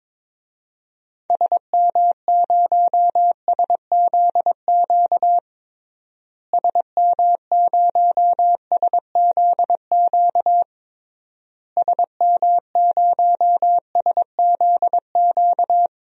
Morse code, heard as a keyed tone, about 22 wpm, 700 Hz. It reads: SM0SZQ SM0SZQ SM0SZQ
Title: DXCC-100 Call Signs - 6 Characters - Repeated 3x - 22wpm